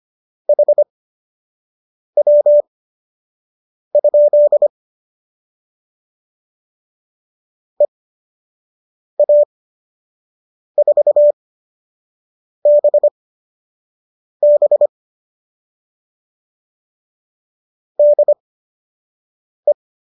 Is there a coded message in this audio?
Yes: Morse code from 0.5 to 19.7 s, heard as a keyed tone, about 25 wpm, spaced as for 6 wpm, 590 Hz. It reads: HW? EA4BB DE